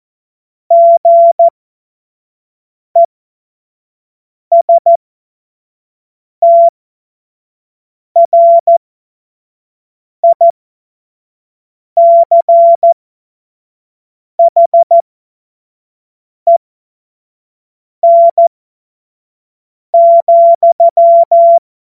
Morse code, heard as a keyed tone, about 14 wpm, spaced as for 5 wpm, 680 Hz. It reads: GESTRICHEN,
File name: Morsekurs / Witze / Witz 12 Dezember 2023 25 F70